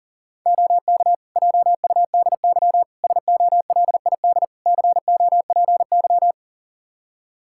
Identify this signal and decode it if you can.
Morse code, heard as a keyed tone, about 37 wpm, 700 Hz. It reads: OK JUDY SOLID COPY
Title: Practice QSO - Exchange 3 - More Info - 40wpm